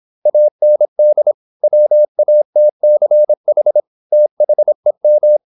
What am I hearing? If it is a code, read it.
Morse code, heard as a keyed tone, about 26 wpm, 600 Hz. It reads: AND WATCH THEM
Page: Never 26wpm | CW med Gnister